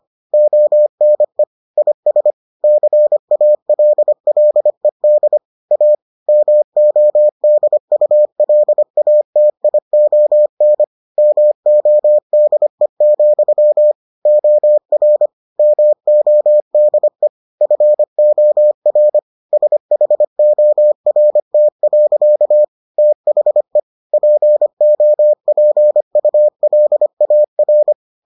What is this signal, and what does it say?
Morse code, heard as a keyed tone, about 25 wpm, 600 Hz. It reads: ONE IS CALLED A MODULATION MODE, OR MODE FOR SHORT. THE POPULAR